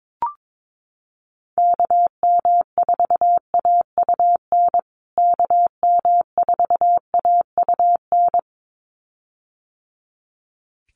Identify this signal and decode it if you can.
Morse code, heard as a keyed tone, about 22 wpm, 700 Hz. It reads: KM4AUN KM4AUN
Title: USA Call Signs Repeated 2x 22wpm